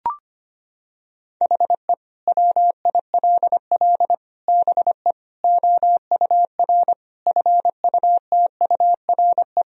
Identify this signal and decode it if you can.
Morse code, heard as a keyed tone, about 25 wpm, 700 Hz. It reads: HE WILL BE OUR FUTURE
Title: Sentences from Top 1000 Words - Review Part 3 - 25wpm